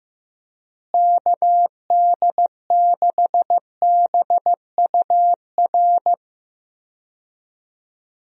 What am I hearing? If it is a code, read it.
Morse code, heard as a keyed tone, about 16 wpm, 700 Hz. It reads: KD6BUR
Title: USA Call Signs - 6 Characters - Repeated 2x - 15wpm